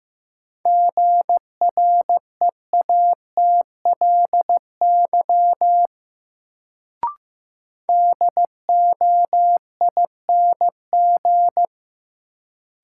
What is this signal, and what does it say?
Morse code, heard as a keyed tone, about 15 wpm, 700 Hz. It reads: GREATLY DOING
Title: Sentences from Top 1400 Words - New Word Drill - 15wpm